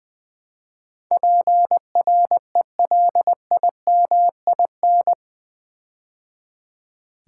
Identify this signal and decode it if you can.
Morse code, heard as a keyed tone, about 20 wpm, 700 Hz. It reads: PRELIMIN
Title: Build A Word - 4 Letters At A Time - 11-Letter Words - 20wpm